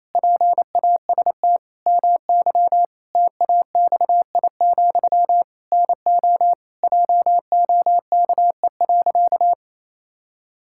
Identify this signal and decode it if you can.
Morse code, heard as a keyed tone, about 28 wpm, 700 Hz. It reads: PAST MY TAXI, NO JOKE.
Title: Pangram Challenge v2 - 28wpm